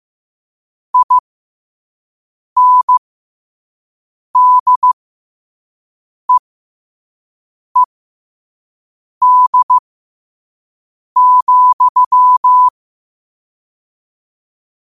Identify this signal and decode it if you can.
Morse code, heard as a keyed tone, about 15 wpm, spaced as for 5 wpm, 1000 Hz. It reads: INDEED,
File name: qotdfiles / 5wpm